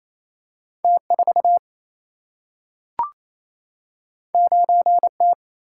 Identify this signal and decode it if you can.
Morse code, heard as a keyed tone, about 28 wpm, 700 Hz. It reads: T4 9T
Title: Single-Letter Single-Number 28wpm